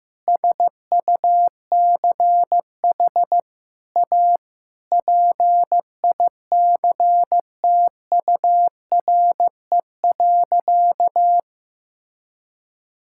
Morse code, heard as a keyed tone, about 15 wpm, 700 Hz. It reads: SUCH A PICTURE.
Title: Sentences from Top 1000 Words - Review Part 6 - 15wpm